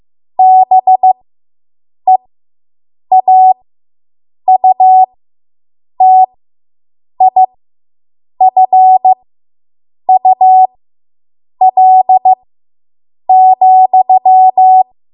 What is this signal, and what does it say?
Morse code, heard as a keyed tone, about 15 wpm, spaced as for 7 wpm, 750 Hz. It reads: BEAUTIFUL,